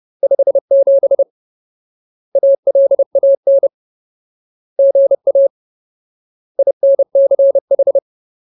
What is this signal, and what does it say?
Morse code, heard as a keyed tone, about 30 wpm, 550 Hz. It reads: 57 ALAN GA INCH
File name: Ultimate Word List / 30WPM Ultimate